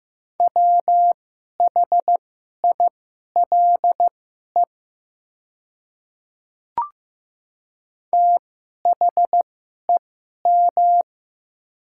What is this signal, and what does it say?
Morse code, heard as a keyed tone, about 15 wpm, spaced as for 11 wpm, 700 Hz. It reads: WHILE THEM
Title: 2x Character Spacing - Top 300 Words - 15wpm